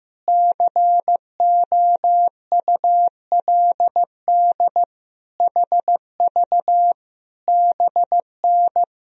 Morse code, heard as a keyed tone, about 15 wpm, 700 Hz. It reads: COULD HV BN
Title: Sentences from Top 500 Words and Abbreviations 15wpm